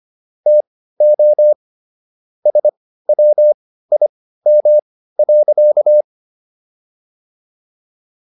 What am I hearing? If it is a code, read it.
Morse code, heard as a keyed tone, about 25 wpm, spaced as for 15 wpm, 600 Hz. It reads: TO SWIM.